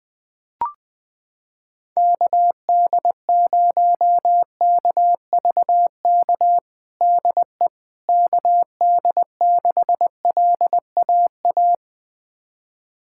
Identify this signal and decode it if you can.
Morse code, heard as a keyed tone, about 20 wpm, 700 Hz. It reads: KD0KVK DE KD6LAA
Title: USA Callsign DE Callsign - 6 Characters - 20wpm